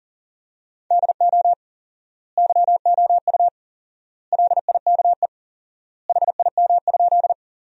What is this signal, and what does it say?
Morse code, heard as a keyed tone, about 40 wpm, 700 Hz. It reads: DO YOU LIKE HIM?
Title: 4x Word-Spacing - Sentences from Top 100 Words - 40wpm